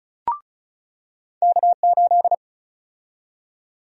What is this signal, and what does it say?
Morse code, heard as a keyed tone, about 34 wpm, 700 Hz. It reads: K8
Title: USA Call Sign Prefixes - 35wpm